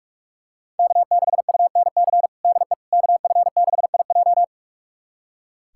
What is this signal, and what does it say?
Morse code, heard as a keyed tone, about 41 wpm, 700 Hz. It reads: K6UNC DE KF6IJ
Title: USA Callsign DE Callsign - 5 Characters - 45wpm